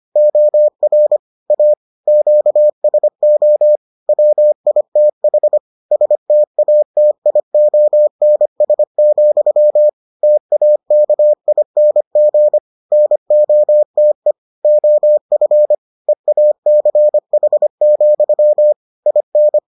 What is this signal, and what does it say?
Morse code, heard as a keyed tone, about 25 wpm, 600 Hz. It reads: OR A QSO WITH STATIONS, TAKING NOTE OF EACH, IN